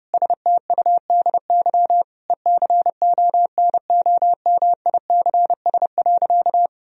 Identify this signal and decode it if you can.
Morse code, heard as a keyed tone, about 30 wpm, 700 Hz. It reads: STUDY ECONOMICS.
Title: Sentences from Top 1000 Words - Review Part 3 - 30wpm